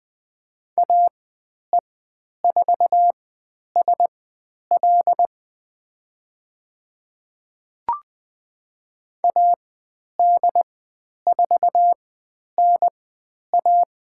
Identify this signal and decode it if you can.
Morse code, heard as a keyed tone, about 20 wpm, spaced as for 10 wpm, 700 Hz. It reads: AE4SL AD4NA
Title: 102 Lesson 17g - Letter u - Call Signs 20wpm